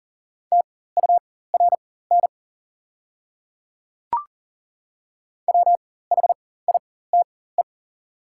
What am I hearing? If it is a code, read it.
Morse code, heard as a keyed tone, about 40 wpm, spaced as for 19 wpm, 690 Hz. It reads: TURN WHITE
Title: 4x Character Spacing - Top 500 Words - 40wpm